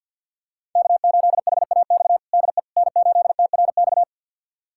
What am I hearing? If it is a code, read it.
Morse code, heard as a keyed tone, about 51 wpm, 700 Hz. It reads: K8HAX DE N8TRX
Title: USA Callsign DE Callsign - 5 Characters - 50wpm